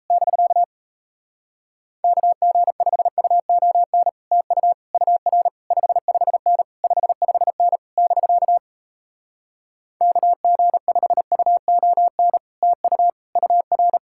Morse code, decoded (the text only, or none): <BK> KG5UON TU UR 55N 55N <BK> KG5UON TU UR